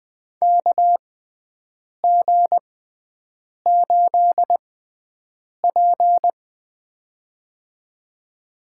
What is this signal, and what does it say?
Morse code, heard as a keyed tone, about 20 wpm, spaced as for 7 wpm, 700 Hz. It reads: KG8P
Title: USA Call Signs - 4 Characters - 6x Character Spacing - 20wpm